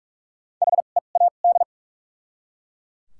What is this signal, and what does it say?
Morse code, heard as a keyed tone, about 45 wpm, 700 Hz. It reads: H E A D
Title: 2x Character Spacing - Top 300 Words - 45wpm